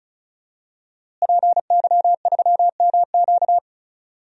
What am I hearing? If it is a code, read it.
Morse code, heard as a keyed tone, about 36 wpm, 700 Hz. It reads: PY3MQ
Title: DXCC-100 Call Signs - 5 Characters - Repeated 2x - 35wpm